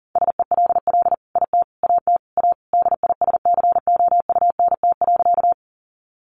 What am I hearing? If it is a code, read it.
Morse code, heard as a keyed tone, about 40 wpm, 700 Hz. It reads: SELL IT AT A DISCOUNT.
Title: Sentences from Top 1200 Words 40wpm